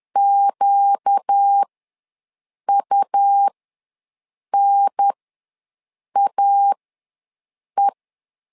Morse code, heard as a keyed tone, about 10 wpm, spaced as for 6 wpm, 800 Hz. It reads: QUNAE
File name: LEZIONI / 4 Q ws 10wpm 2306